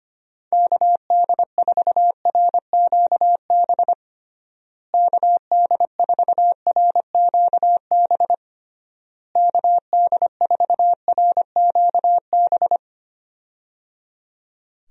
Morse code, heard as a keyed tone, about 25 wpm, 700 Hz. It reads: KD4RQB KD4RQB KD4RQB
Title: USA Call Signs - 6 Characters - Repeated 3x - 25wpm